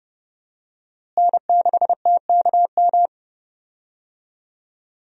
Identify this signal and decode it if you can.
Morse code, heard as a keyed tone, about 30 wpm, 700 Hz. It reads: N6TKM
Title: USA Call Signs - 5 Characters - Repeated 2x - 30wpm